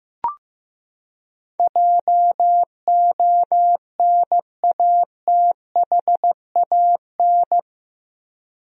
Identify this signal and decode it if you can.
Morse code, heard as a keyed tone, about 15 wpm, 700 Hz. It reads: JONATHAN
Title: USA Common Names - 15wpm